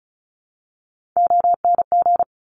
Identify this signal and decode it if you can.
Morse code, heard as a keyed tone, about 36 wpm, 700 Hz. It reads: ONG
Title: Top Three Letter Combinations in Words 35wpm